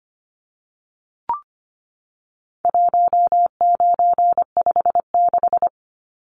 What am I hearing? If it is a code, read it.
Morse code, heard as a keyed tone, about 25 wpm, 700 Hz. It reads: 1956
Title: Numbers - 4-Digit 25wpm